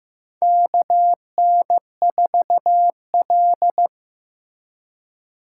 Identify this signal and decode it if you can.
Morse code, heard as a keyed tone, about 15 wpm, 700 Hz. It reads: KN4L
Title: USA Call Signs - 4x Character Spacing - 15wpm